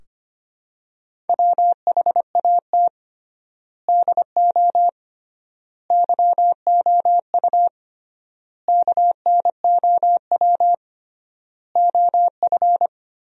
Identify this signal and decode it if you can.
Morse code, heard as a keyed tone, about 25 wpm, 700 Hz. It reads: WHAT DO YOU KNOW OF
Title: 3x Word-Spacing - Sentences from Top 100 Words - 25wpm